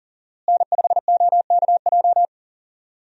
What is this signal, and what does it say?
Morse code, heard as a keyed tone, about 40 wpm, 700 Hz. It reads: N5OKJ
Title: USA Call Signs - 5 Characters - Repeated 2x - 40wpm